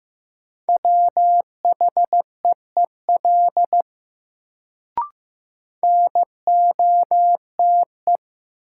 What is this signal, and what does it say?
Morse code, heard as a keyed tone, about 15 wpm, 700 Hz. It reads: WHEEL NOTE
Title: Sentences from Top 500 Words - New Word Drill - 15wpm